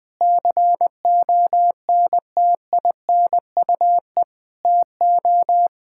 Morse code, heard as a keyed tone, about 20 wpm, 700 Hz. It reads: CONTINUE TO
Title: Sentences from Top 1600 Words 20wpm